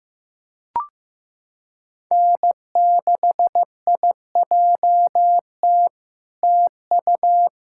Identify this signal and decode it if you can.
Morse code, heard as a keyed tone, about 15 wpm, 700 Hz. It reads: N6IJT TU
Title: SOTA QSO Practice 15wpm